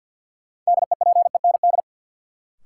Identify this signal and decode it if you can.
Morse code, heard as a keyed tone, about 52 wpm, 700 Hz. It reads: DEPEND
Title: Sentences from Top 1300 Words - New Word Drill Alt - 50wpm